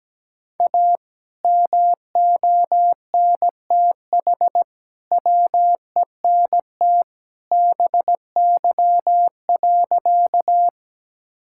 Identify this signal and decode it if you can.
Morse code, heard as a keyed tone, about 17 wpm, 700 Hz. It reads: A MONTH WENT BY.
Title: Sentences from Top 1000 Words - Review Part 5 - 17wpm